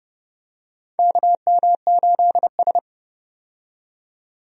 Morse code, heard as a keyed tone, about 29 wpm, 700 Hz. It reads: KM8S
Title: USA Call Signs - 4 Characters - 30wpm